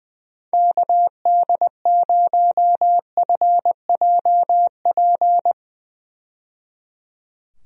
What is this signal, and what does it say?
Morse code, heard as a keyed tone, about 20 wpm, 700 Hz. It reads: KD0FJP
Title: USA Call Signs - 6 Characters - 20wpm